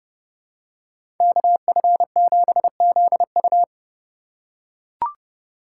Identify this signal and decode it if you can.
Morse code, heard as a keyed tone, about 32 wpm, 700 Hz. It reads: KF7ZU
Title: USA Call Signs - 5 Characters - 4x Character Spacing - 30wpm